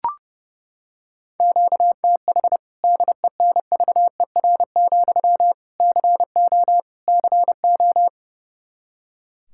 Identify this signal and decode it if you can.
Morse code, heard as a keyed tone, about 30 wpm, 700 Hz. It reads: QTH DENVER, CO CO
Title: K1USN SST QSO Practice 30wpm